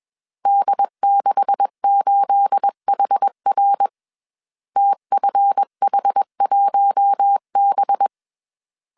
Morse code, heard as a keyed tone, about 21 wpm, 800 Hz. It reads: D68HR TFH1B
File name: CW / DET / 1 9 1500C S C 20wpm 2310